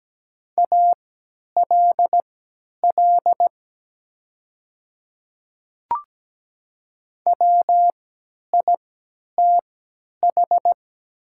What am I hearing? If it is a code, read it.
Morse code, heard as a keyed tone, about 17 wpm, spaced as for 10 wpm, 700 Hz. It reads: ALL WITH
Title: 3x Character Spacing - Top 100 Words - 17wpm